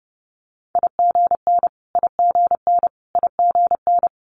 Morse code, heard as a keyed tone, about 30 wpm, 700 Hz. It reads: IGN IGN IGN
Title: Multi-letter Phonemes Repeated 3x 30wpm